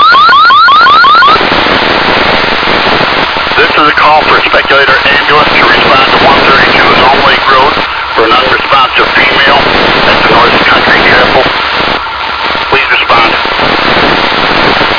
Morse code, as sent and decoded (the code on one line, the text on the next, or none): none